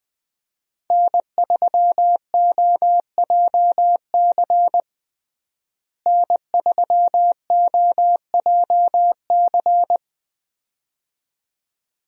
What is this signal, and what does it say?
Morse code, heard as a keyed tone, about 20 wpm, 700 Hz. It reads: N3OJC N3OJC
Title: USA Call Signs - 5 Characters - Repeated 2x - 20wpm